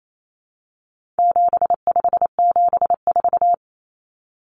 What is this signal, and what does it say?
Morse code, heard as a keyed tone, about 28 wpm, 700 Hz. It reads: 7574
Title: Common Cut Numbers - 4-Digit 28wpm